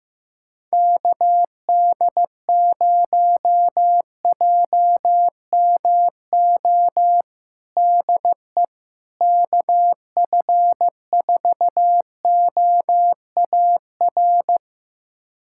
Morse code, heard as a keyed tone, about 16 wpm, 700 Hz. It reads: KD0JMO DE KF4OAR
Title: USA Callsign DE Callsign - 6 Characters - 15wpm